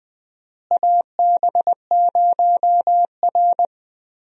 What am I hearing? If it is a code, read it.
Morse code, heard as a keyed tone, about 20 wpm, 700 Hz. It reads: AB0R
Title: USA Call Signs - 4 Characters - Repeated 2x - 20wpm